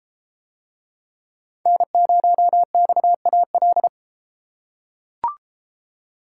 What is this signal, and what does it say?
Morse code, heard as a keyed tone, about 33 wpm, 700 Hz. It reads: N0XAL